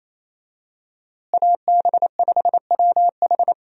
Morse code, heard as a keyed tone, about 27 wpm, 700 Hz. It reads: AB5WH
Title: USA Call Signs - 5 Characters - Repeated 2x - 28wpm